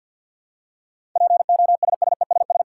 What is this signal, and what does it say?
Morse code, heard as a keyed tone, about 53 wpm, 700 Hz. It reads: POSSESS
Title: Seven Letter Words - 50wpm